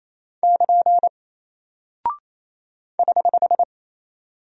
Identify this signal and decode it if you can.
Morse code, heard as a keyed tone, about 28 wpm, 700 Hz. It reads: <KN> <HH>
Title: Single Character (Letter:Number:Punctuation:Prosign) Alt 28wpm